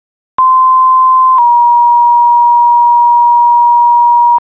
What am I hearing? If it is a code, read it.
Two-tone page, heard as a one-second tone, then a three-second tone, about 1040 then 950 Hz. It is heard from 0.4 to 4.4 s.